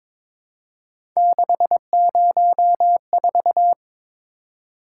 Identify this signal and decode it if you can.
Morse code, heard as a keyed tone, about 22 wpm, 700 Hz. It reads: 604
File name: practice / 22wpm / Three Letter-Number Combinations - 22wpm